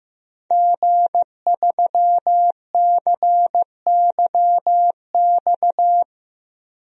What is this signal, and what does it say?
Morse code, heard as a keyed tone, about 15 wpm, 700 Hz. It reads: G3CYX
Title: DXCC-100 Call Signs - 5 Characters - 15wpm